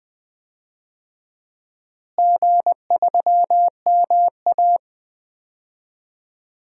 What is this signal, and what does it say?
Morse code, heard as a keyed tone, about 20 wpm, 700 Hz. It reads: G3MA
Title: DXCC-100 Call Signs - 4 Characters - Repeated 2x - 20wpm